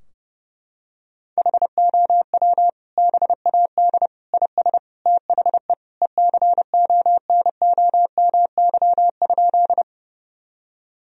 Morse code, heard as a keyed tone, about 30 wpm, 700 Hz. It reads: HOW BAD IS THE ECONOMY?
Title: Sentences from Top 1000 Words - Review Part 3 - 30wpm